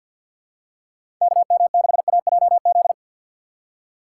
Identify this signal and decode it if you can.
Morse code, heard as a keyed tone, about 52 wpm, 700 Hz. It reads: KM6AJZ
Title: USA Call Signs - 6 Characters - 6x Character Spacing - 50wpm